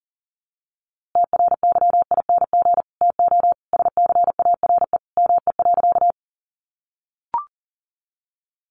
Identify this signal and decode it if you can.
Morse code, heard as a keyed tone, about 40 wpm, 700 Hz. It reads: TRYING TO SCARE ME.
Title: A Sound Of Thunder 40wpm